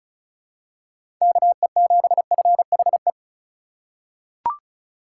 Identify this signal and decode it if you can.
Morse code, heard as a keyed tone, about 36 wpm, 700 Hz. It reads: KE7FHE
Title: USA Call Signs - 6 Characters - 4x Character Spacing - 35wpm